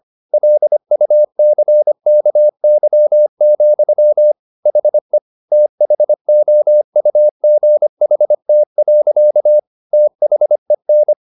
Morse code, heard as a keyed tone, about 25 wpm, 600 Hz. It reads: LUCKY, HE THOUGHT. THEN